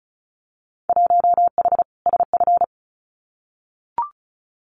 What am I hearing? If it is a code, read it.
Morse code, heard as a keyed tone, about 34 wpm, 700 Hz. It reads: 1H SF
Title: Winter Field Day Exchanges 35wpm